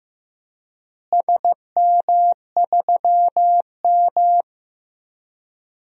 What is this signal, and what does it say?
Morse code, heard as a keyed tone, about 15 wpm, 700 Hz. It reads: SM3M
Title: DXCC-100 Call Signs - 4 Characters - Repeated 3x - 15wpm